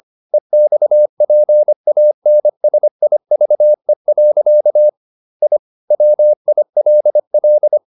Morse code, heard as a keyed tone, about 25 wpm, 600 Hz. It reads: EXPANSIVE. I WILL